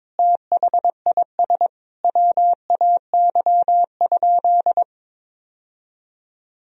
Morse code, heard as a keyed tone, about 22 wpm, 700 Hz. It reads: THIS WAY?
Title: Sentences from Top 1000 Words - Review Part 1 - 22wpm